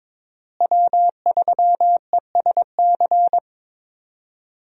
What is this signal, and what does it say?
Morse code, heard as a keyed tone, about 22 wpm, 700 Hz. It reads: W3ESC
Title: USA Call Signs - 5 Characters - Repeated 2x - 22wpm